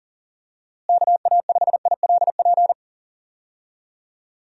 Morse code, heard as a keyed tone, about 40 wpm, 700 Hz. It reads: KA5ILP
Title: USA Call Signs - 6 Characters - Repeated 2x - 40wpm